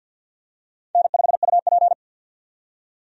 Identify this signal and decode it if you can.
Morse code, heard as a keyed tone, about 48 wpm, 700 Hz. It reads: N5UP
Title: USA Call Signs - 4 Characters - 50wpm